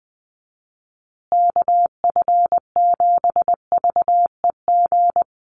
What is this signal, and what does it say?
Morse code, heard as a keyed tone, about 20 wpm, 700 Hz. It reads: KF7VEG
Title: USA Call Signs Repeated 2x 20wpm